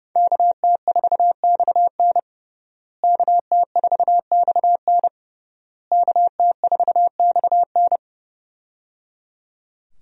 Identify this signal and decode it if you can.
Morse code, heard as a keyed tone, about 29 wpm, 700 Hz. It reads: KT4XN KT4XN KT4XN
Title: USA Call Signs - 5 Characters - Repeated 3x - 30wpm